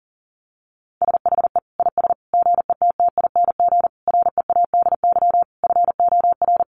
Morse code, heard as a keyed tone, about 40 wpm, 700 Hz. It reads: SHE IS GETTING READY FOR
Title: Easy Phrases 40wpm